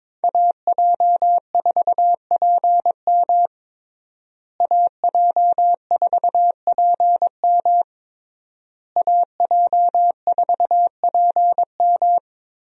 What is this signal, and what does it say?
Morse code, heard as a keyed tone, about 22 wpm, 700 Hz. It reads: AJ4PM AJ4PM AJ4PM
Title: USA Call Signs - 5 Characters - Repeated 3x - 22wpm